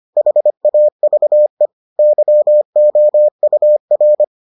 Morse code, heard as a keyed tone, about 25 wpm, 600 Hz. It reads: HAVE YOUR